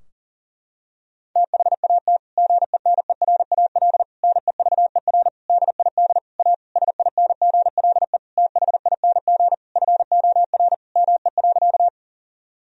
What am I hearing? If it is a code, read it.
Morse code, heard as a keyed tone, about 40 wpm, 700 Hz. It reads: THAT GENERAL NEVER DID A SINGLE THING FOR ME.
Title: Sentences from Top 1000 Words - Review Part 4 - 40wpm